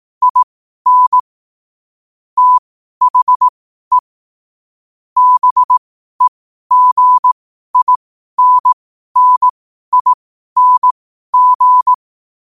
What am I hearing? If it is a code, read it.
Morse code, heard as a keyed tone, about 18 wpm, spaced as for 12 wpm, 1000 Hz. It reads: IN THE BEGINNING
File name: qotdfiles / 12wpm